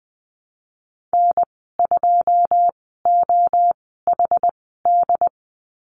Morse code, heard as a keyed tone, about 20 wpm, spaced as for 14 wpm, 700 Hz. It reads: N2OHD
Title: 128 Lesson 21g - Number 6 - Call Signs 20wpm